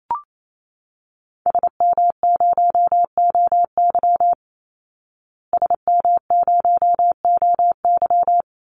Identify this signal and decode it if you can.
Morse code, heard as a keyed tone, about 28 wpm, 690 Hz. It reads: SM0OY SM0OY
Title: Call Signs Repeated 3x - 28wpm